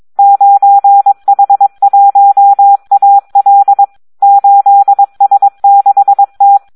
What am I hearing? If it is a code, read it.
Morse code, heard as a keyed tone, about 22 wpm, 800 Hz. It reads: 9H1AL 8S6T
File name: Audio practica CW / SD 1 22